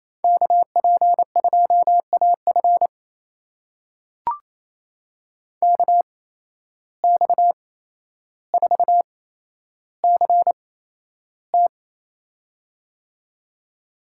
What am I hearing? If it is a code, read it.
Morse code, heard as a keyed tone, about 28 wpm, spaced as for 12 wpm, 700 Hz. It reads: KP2AF KX4CT
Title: USA Call Signs - 5 Characters - 8x Character Spacing - 28wpm